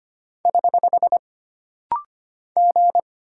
Morse code, heard as a keyed tone, about 25 wpm, 700 Hz. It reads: <HH> G
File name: Single Character (Letter:Number:Punctuation:Prosign) - Rapid-Fire - 25wpm